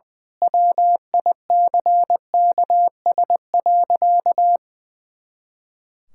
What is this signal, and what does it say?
Morse code, heard as a keyed tone, about 20 wpm, 700 Hz. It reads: WICKS.
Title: Pangram Challenge v2 - 20wpm